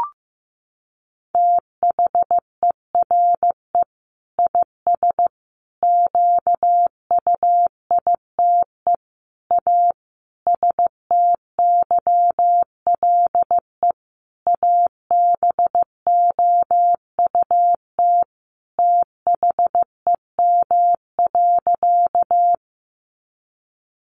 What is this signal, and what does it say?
Morse code, heard as a keyed tone, about 15 wpm, 700 Hz. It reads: THERE IS QUITE A STYLE ABOUT THEM.
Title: Sentences from Top 800 Words 15wpm